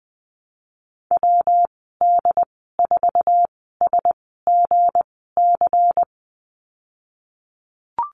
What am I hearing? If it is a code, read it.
Morse code, heard as a keyed tone, about 20 wpm, spaced as for 15 wpm, 700 Hz. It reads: WD4SGC
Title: 173 Lesson 28g - Number 9 - Call Signs 20wpm